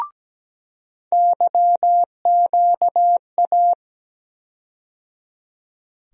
Morse code, heard as a keyed tone, about 17 wpm, 700 Hz. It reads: YQA
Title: USA Call Sign Postfixes - 3 Letters 17wpm